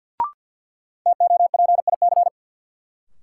Morse code, heard as a keyed tone, about 53 wpm, 700 Hz. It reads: TOPIC
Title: Sentences from Top 1200 Words - New Word Drill - 50wpm